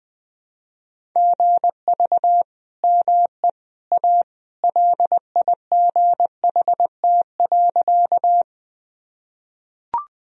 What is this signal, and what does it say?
Morse code, heard as a keyed tone, about 21 wpm, 700 Hz. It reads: GV ME A LIGHT.
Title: Sentences from Top 200 Words and Abbreviations 20wpm